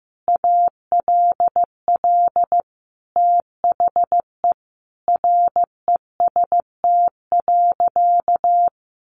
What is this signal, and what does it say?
Morse code, heard as a keyed tone, about 15 wpm, 700 Hz. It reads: ALL THE REST.
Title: Sentences from Top 1300 Words 15wpm